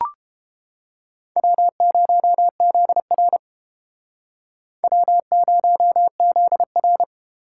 Morse code, heard as a keyed tone, about 32 wpm, 700 Hz. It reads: W0ZR W0ZR